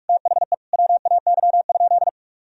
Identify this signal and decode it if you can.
Morse code, heard as a keyed tone, about 46 wpm, 700 Hz. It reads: THE WAY?
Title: Sentences from Top 1000 Words - Review Part 1 - 45wpm